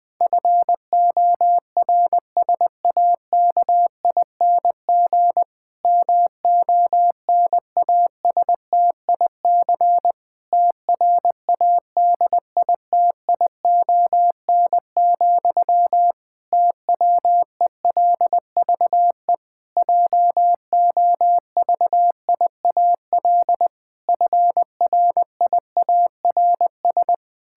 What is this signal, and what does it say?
Morse code, heard as a keyed tone, about 20 wpm, 700 Hz. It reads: FORSAKING MONASTIC TRADITION, TWELVE JOVIAL FRIARS
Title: Pangram Challenge v2 - 20wpm